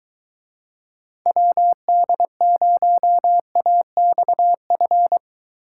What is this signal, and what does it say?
Morse code, heard as a keyed tone, about 23 wpm, 700 Hz. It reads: WD0AXF